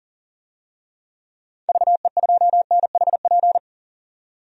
Morse code, heard as a keyed tone, about 40 wpm, 700 Hz. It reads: VE2NHP
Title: DXCC-100 Call Signs - 6 Characters - 40wpm